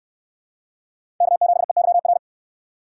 Morse code, heard as a keyed tone, about 69 wpm, 700 Hz. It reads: K7EYR